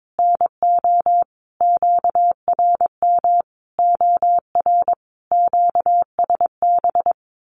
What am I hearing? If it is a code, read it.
Morse code, heard as a keyed tone, about 22 wpm, 700 Hz. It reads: NO QRM OR QSB
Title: Common QSO Phrases 22wpm